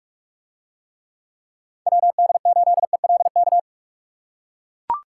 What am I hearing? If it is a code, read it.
Morse code, heard as a keyed tone, about 44 wpm, 700 Hz. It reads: WD8ELK